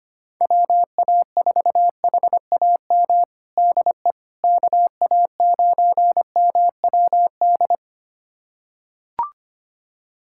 Morse code, heard as a keyed tone, about 25 wpm, 700 Hz. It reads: WA4HAM DE KA9MWD
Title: USA Callsign DE Callsign - 6 Characters - 25wpm